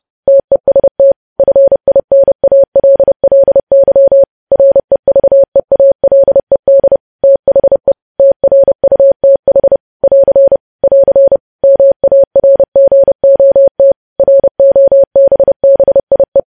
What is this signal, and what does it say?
Morse code, heard as a keyed tone, about 30 wpm, 550 Hz. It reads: TEST FINALLY REVEALED THE TRUTH <AR> <AR> MARGOT ROBBIE
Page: news headlines in morse code at 30 wpm